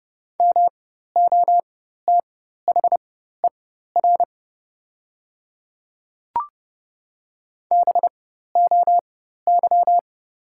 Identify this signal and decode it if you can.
Morse code, heard as a keyed tone, about 30 wpm, spaced as for 14 wpm, 700 Hz. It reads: MOTHER BOY
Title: 4x Character Spacing - Top 500 Words - 30wpm